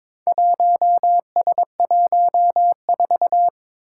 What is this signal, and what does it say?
Morse code, heard as a keyed tone, about 22 wpm, 700 Hz. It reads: 1S14
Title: Four Challenging Letter-Number Combinations 22wpm